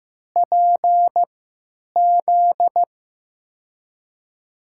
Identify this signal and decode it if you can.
Morse code, heard as a keyed tone, about 15 wpm, spaced as for 9 wpm, 700 Hz. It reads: PZ
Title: USA Call Signs - 5 Characters - 3x Character Spacing - 15wpm